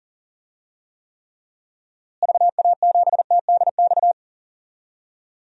Morse code, heard as a keyed tone, about 40 wpm, 700 Hz. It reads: VA7TDX
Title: DXCC-100 Call Signs - 6 Characters - Repeated 2x - 40wpm